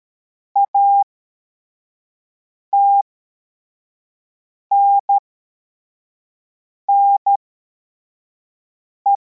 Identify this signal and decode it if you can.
Morse code, heard as a keyed tone, about 13 wpm, spaced as for 4 wpm, 800 Hz. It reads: ATNNE
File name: formation / cw / Text21